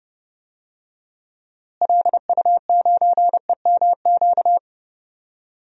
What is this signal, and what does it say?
Morse code, heard as a keyed tone, about 30 wpm, 700 Hz. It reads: LU9EMQ
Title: DXCC-100 Call Signs - 6 Characters - Repeated 2x - 30wpm